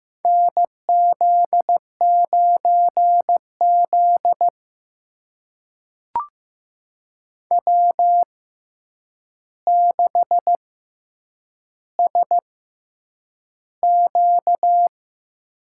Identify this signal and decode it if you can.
Morse code, heard as a keyed tone, about 15 wpm, spaced as for 8 wpm, 700 Hz. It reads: NZ9Z W6SQ
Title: USA Call Signs - 4 Characters - 6x Character Spacing - 15wpm